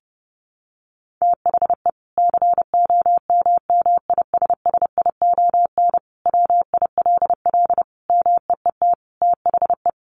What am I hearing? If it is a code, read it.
Morse code, heard as a keyed tone, about 30 wpm, 700 Hz. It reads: THE COMMISSION WILL MEET THE
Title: Sentences from Top 1400 Words 30wpm